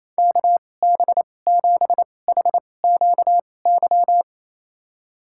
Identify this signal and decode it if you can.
Morse code, heard as a keyed tone, about 28 wpm, spaced as for 20 wpm, 700 Hz. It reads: KB7HQY
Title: USA Call Signs - 6 Characters - 2x Character Spacing - 28wpm